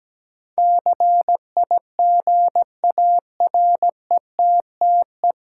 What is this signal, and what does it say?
Morse code, heard as a keyed tone, about 17 wpm, 700 Hz. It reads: CIGARETTE
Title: Sentences from Top 1000 Words - New Word Drill - 17wpm